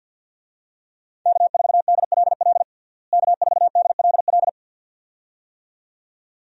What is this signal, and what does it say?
Morse code, heard as a keyed tone, about 48 wpm, 700 Hz. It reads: K4DLL K4DLL
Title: USA Call Signs - 5 Characters - Repeated 2x - 50wpm